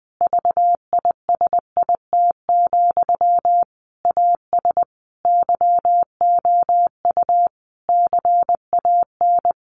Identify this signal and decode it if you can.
Morse code, heard as a keyed tone, about 20 wpm, 700 Hz. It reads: VISIT, AS YOU CAN
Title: Sentences from Top 1400 Words 20wpm